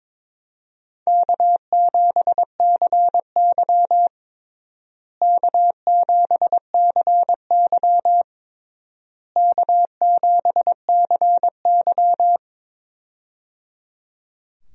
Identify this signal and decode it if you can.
Morse code, heard as a keyed tone, about 22 wpm, 700 Hz. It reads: K7CY K7CY K7CY
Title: USA Call Signs - 4 Characters - Repeated 3x - 22wpm